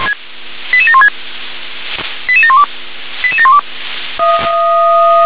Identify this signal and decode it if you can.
Five-tone selective call (ZVEI), heard as quick runs of five tones, tones of about 70 ms. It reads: 88716 88712 88712